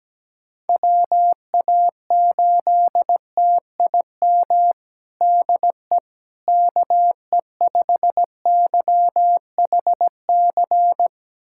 Morse code, heard as a keyed tone, about 17 wpm, 700 Hz. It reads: WA8TIM DE KE5YHC
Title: USA Callsign DE Callsign 17wpm